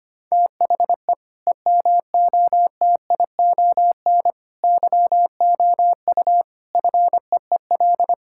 Morse code, heard as a keyed tone, about 25 wpm, 700 Hz. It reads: THE EMOTION YOU FEEL
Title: Sentences from Top 1600 Words 25wpm